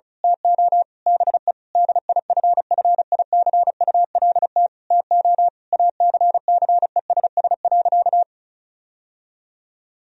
Morse code, heard as a keyed tone, about 34 wpm, 700 Hz. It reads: TO BE DIFFICULT TO ACCESS.
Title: Sentences from Top 1000 Words - Review Part 4 - 35wpm